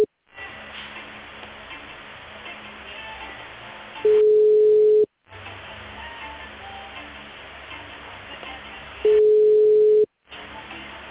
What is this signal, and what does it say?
Ringback tone (Europe, 425 Hz), with speech.